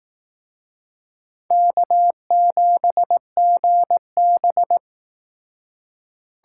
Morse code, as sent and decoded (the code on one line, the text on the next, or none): -.- --... --. -...
K7GB